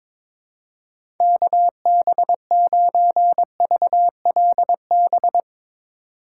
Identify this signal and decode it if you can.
Morse code, heard as a keyed tone, about 22 wpm, 700 Hz. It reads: KB9VLB
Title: USA Call Signs - 6 Characters - 2x Character Spacing - 22wpm